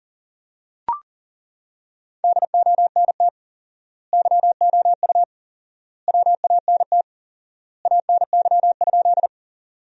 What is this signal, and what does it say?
Morse code, heard as a keyed tone, about 40 wpm, 700 Hz. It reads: DONT YOU WANT ANY?
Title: 4x Word-Spacing - Sentences from Top 300 Words - 40wpm